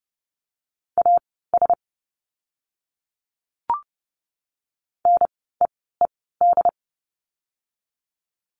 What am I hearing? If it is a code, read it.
Morse code, heard as a keyed tone, about 30 wpm, spaced as for 17 wpm, 700 Hz. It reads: AS NEED
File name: course / 30:10wpm / 115 Lesson 19f - Letter w - Words 30wpm